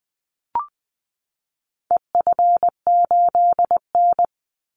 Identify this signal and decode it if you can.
Morse code, heard as a keyed tone, about 20 wpm, 700 Hz. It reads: EF8N
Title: Call Signs Repeated 3x - 20wpm